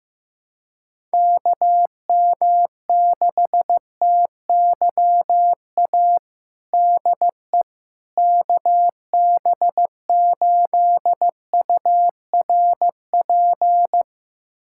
Morse code, heard as a keyed tone, about 15 wpm, 700 Hz. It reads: KM6TYA DE KB8URP
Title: USA Callsign DE Callsign 15wpm